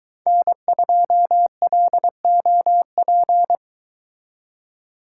Morse code, heard as a keyed tone, about 22 wpm, 700 Hz. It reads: N2LOP